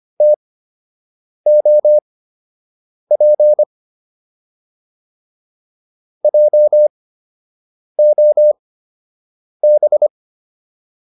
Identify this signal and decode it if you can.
Morse code, heard as a keyed tone, about 25 wpm, spaced as for 7 wpm, 600 Hz. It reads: TOP JOB